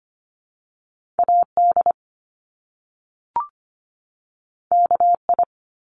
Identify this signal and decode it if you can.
Morse code, heard as a keyed tone, about 25 wpm, 700 Hz. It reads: AD KI
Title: Top Two Letter Combinations in Words 25wpm